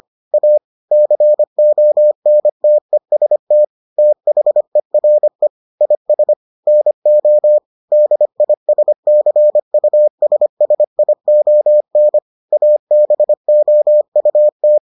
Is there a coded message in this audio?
Yes: Morse code from 0.3 to 14.8 s, heard as a keyed tone, about 25 wpm, 600 Hz. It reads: A CONTEST THERE IS NO DISCUSSION ABOUT